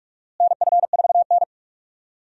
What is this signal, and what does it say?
Morse code, heard as a keyed tone, about 43 wpm, 700 Hz. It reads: NF4N
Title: USA Call Signs - 4 Characters - Repeated 2x - 45wpm